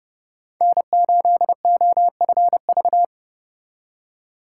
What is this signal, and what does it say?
Morse code, heard as a keyed tone, about 40 wpm, 700 Hz. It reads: N8OFV